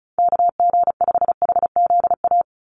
Morse code, heard as a keyed tone, about 34 wpm, 700 Hz. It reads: KG5HZA
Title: USA Call Signs 35wpm